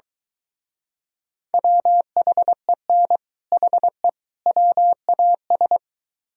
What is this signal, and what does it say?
Morse code, heard as a keyed tone, about 23 wpm, 700 Hz. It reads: WHEN HE WAS